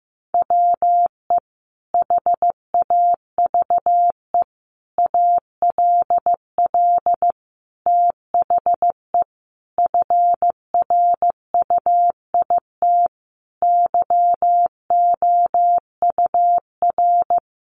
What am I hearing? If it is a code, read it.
Morse code, heard as a keyed tone, about 15 wpm, 700 Hz. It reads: WE HAVE ALL THE FRUIT YOUR
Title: Sentences from Top 1300 Words 15wpm